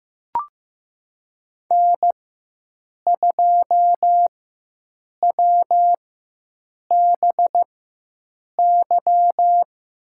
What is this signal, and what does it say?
Morse code, heard as a keyed tone, about 16 wpm, spaced as for 7 wpm, 700 Hz. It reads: N2WBY
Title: USA Call Signs - 5 Characters - 4x Character Spacing - 15wpm